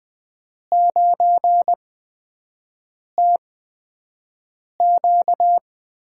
Morse code, heard as a keyed tone, about 20 wpm, spaced as for 5 wpm, 700 Hz. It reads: 9TQ
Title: USA Call Signs - 5 Characters - 8x Character Spacing - 20wpm